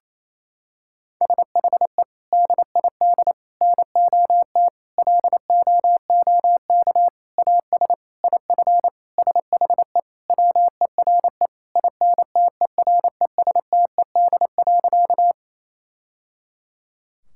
Morse code, heard as a keyed tone, about 28 wpm, 700 Hz. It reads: SHE DID NOT LOOK AS IF SHE WERE INTERESTED.
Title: Sentences from Top 1000 Words - Review Part 6 - 28wpm